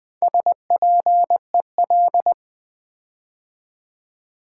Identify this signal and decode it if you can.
Morse code, heard as a keyed tone, about 20 wpm, 700 Hz. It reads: SPEL
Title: Build A Word - 4 Letters At A Time - 8-Letter Words - 20wpm